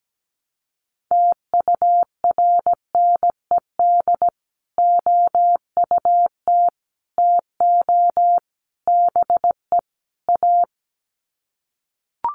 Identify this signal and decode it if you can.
Morse code, heard as a keyed tone, about 17 wpm, 700 Hz. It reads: TURNED OUT TO BE A
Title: Sets of 5 Words 17wpm